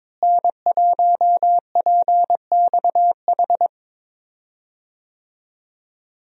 Morse code, heard as a keyed tone, about 21 wpm, 700 Hz. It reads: N1PXH
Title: USA Call Signs - 5 Characters - Repeated 3x - 22wpm